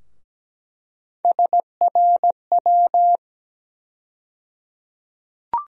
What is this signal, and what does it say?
Morse code, heard as a keyed tone, about 17 wpm, 700 Hz. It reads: SRW
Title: USA Call Sign Postfixes - 3 Letters 17wpm